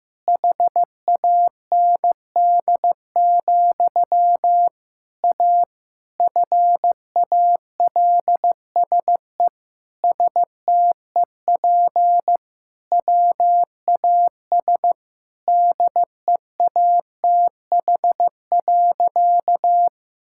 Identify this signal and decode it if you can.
Morse code, heard as a keyed tone, about 15 wpm, 700 Hz. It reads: HAND, A FALSE STEP WAS DEATH.
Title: Sentences from Top 1600 Words 15wpm